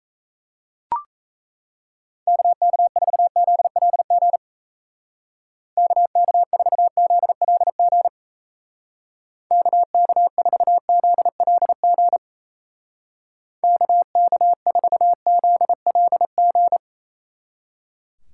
Morse code, heard as a keyed tone, about 35 wpm, 700 Hz. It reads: KK4ZLG KK4ZLG KK4ZLG KK4ZLG